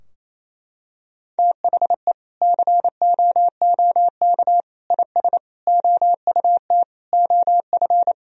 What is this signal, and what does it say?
Morse code, heard as a keyed tone, about 28 wpm, 700 Hz. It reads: THE COOK IS OUT OF
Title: Sentences from Top 1000 Words - Review Part 4 - 28wpm